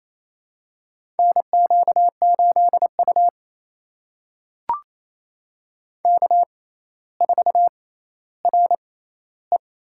Morse code, heard as a keyed tone, about 28 wpm, spaced as for 14 wpm, 700 Hz. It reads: NQ8U K4RE
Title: USA Call Signs - 4 Characters - 6x Character Spacing - 28wpm